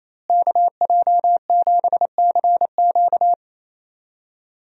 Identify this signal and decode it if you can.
Morse code, heard as a keyed tone, about 28 wpm, 700 Hz. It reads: KJ7CQ
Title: USA Call Signs - 5 Characters - Repeated 3x - 28wpm